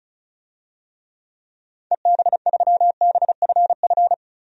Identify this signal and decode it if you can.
Morse code, heard as a keyed tone, about 36 wpm, 700 Hz. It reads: EB3BFF
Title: DXCC-100 Call Signs - 6 Characters - Repeated 3x - 35wpm